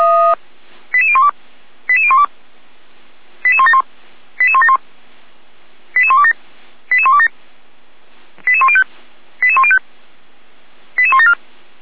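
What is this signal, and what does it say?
Five-tone selective call (ZVEI), heard as quick runs of five tones, tones of about 70 ms. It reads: 88012 88012 88171 88171 88127 88127 88105 88105 88174
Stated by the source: ZVEI five-tone selective call received on an SDR, logged by Feuerwehr Orlamünde (f85360).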